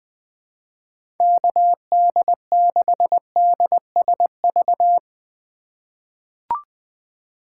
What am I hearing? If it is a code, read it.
Morse code, heard as a keyed tone, about 20 wpm, 700 Hz. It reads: KD6DSV
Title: USA Call Signs - 3x Character Spacing - 20wpm